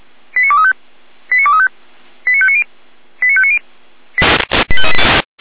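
Five-tone selective call (ZVEI), heard as quick runs of five tones, tones of about 70 ms. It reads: 89235 89235 89590 89590
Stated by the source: ZVEI five-tone call received by SDR, from the feuerwehr-orlamuende.org archive.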